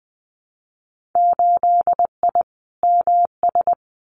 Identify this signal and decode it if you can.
Morse code, heard as a keyed tone, about 20 wpm, 700 Hz. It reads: 8I MS
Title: Winter Field Day Exchanges Encore 20wpm